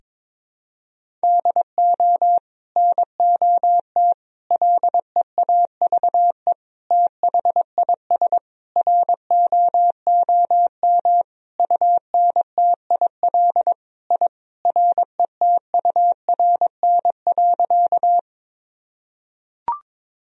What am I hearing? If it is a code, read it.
Morse code, heard as a keyed tone, about 22 wpm, 700 Hz. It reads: DO NOT LEAVE THIS ROOM UNTIL I RETURN.
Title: Sentences from Top 1000 Words - Review Part 1 - 22wpm